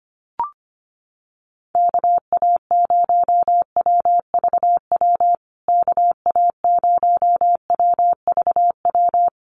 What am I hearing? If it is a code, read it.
Morse code, heard as a keyed tone, about 25 wpm, 700 Hz. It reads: KA0WVW KA0WVW
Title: USA Call Signs Repeated 3x 25wpm